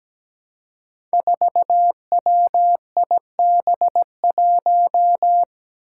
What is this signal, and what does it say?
Morse code, heard as a keyed tone, about 17 wpm, 700 Hz. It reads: 4WIB1
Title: Five Challenging Letter-Number Combinations 17wpm